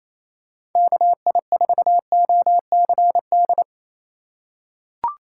Morse code, heard as a keyed tone, about 28 wpm, 700 Hz. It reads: KI4OCD
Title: USA Call Signs - 6 Characters - 3x Character Spacing - 28wpm